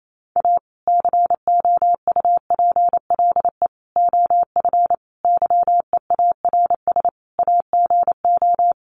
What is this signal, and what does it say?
Morse code, heard as a keyed tone, about 28 wpm, 700 Hz. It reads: A COUPLE OF YEARS AGO
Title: Sets of 5 Words 28wpm